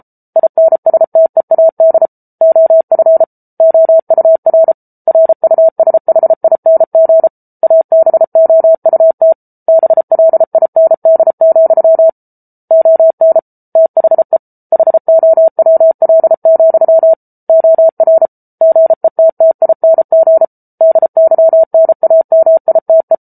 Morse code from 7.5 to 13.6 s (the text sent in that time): ABOUT BLIND, ON